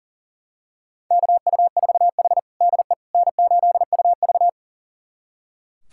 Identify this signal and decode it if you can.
Morse code, heard as a keyed tone, about 37 wpm, 700 Hz. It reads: KU4H DE N8UV
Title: USA Callsign DE Callsign - 4 Characters - 40wpm